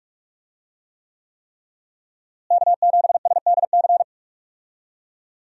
Morse code, heard as a keyed tone, about 45 wpm, 700 Hz. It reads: K7SDC
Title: USA Call Signs - 5 Characters - Repeated 3x - 45wpm